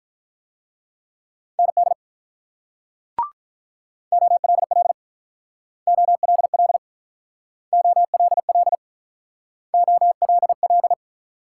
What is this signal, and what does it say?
Morse code, heard as a keyed tone, about 46 wpm, 700 Hz. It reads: ND OLL OLL OLL OLL